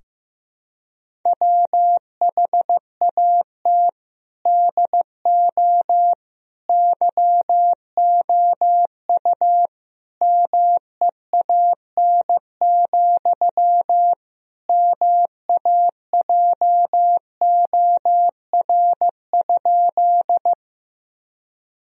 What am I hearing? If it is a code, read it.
Morse code, heard as a keyed tone, about 15 wpm, 700 Hz. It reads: WHAT DO YOU MEAN, MAJOR?
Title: Sentences from Top 1000 Words - Review Part 4 - 15wpm